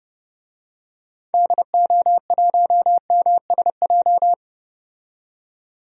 Morse code, heard as a keyed tone, about 30 wpm, 700 Hz. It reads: DO1MSJ
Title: DXCC-100 Call Signs - 6 Characters - 30wpm